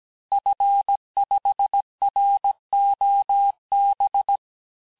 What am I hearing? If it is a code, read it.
Morse code, heard as a keyed tone, about 17 wpm, 800 Hz. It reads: F5ROB
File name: cours-du-lundi files / cours-1587-FIN-de-QSO-vitesse-16MPM